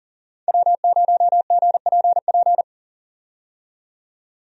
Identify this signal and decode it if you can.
Morse code, heard as a keyed tone, about 40 wpm, 700 Hz. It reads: W0GPP
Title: USA Call Signs - 5 Characters - 40wpm